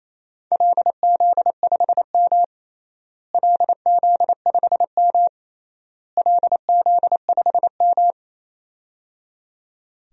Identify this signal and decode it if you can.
Morse code, heard as a keyed tone, about 28 wpm, 700 Hz. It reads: LZ5M LZ5M LZ5M
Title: DXCC-100 Call Signs - 4 Characters - Repeated 3x - 28wpm